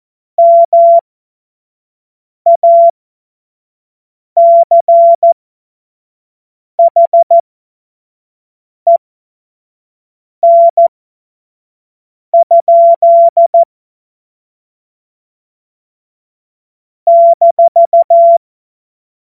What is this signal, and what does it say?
Morse code, heard as a keyed tone, about 14 wpm, spaced as for 5 wpm, 680 Hz. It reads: MACHEN? -
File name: Morsekurs / Witze / Witz 23 April 2024 25 F70